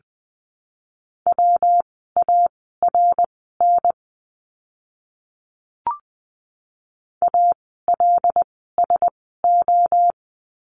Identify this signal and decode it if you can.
Morse code, heard as a keyed tone, about 20 wpm, spaced as for 14 wpm, 700 Hz. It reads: WARN ALSO
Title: 115 Lesson 19f - Letter w - Words 20wpm